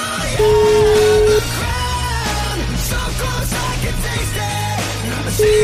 Ringback tone (Europe, 425 Hz), with noise.